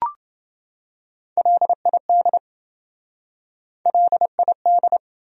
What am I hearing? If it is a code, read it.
Morse code, heard as a keyed tone, about 27 wpm, 700 Hz. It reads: LID LID